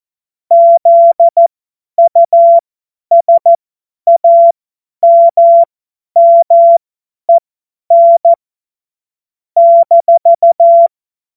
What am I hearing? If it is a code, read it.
Morse code, heard as a keyed tone, about 14 wpm, spaced as for 10 wpm, 680 Hz. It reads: ZUSAMMEN -
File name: Witz-05-August-2025-50-F70